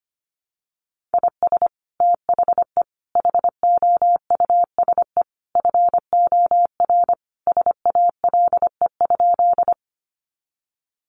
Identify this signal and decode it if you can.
Morse code, heard as a keyed tone, about 25 wpm, 700 Hz. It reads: IS THE HOUSE FOR SALE?
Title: Sentences from Top 1400 Words 25wpm